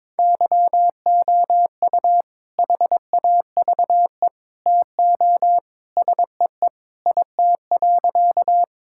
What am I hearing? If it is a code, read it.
Morse code, heard as a keyed tone, about 22 wpm, 700 Hz. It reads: YOU HAVE TO SEE IT.
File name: practice / 22wpm / Sentences with 2-Word Phrases - 22wpm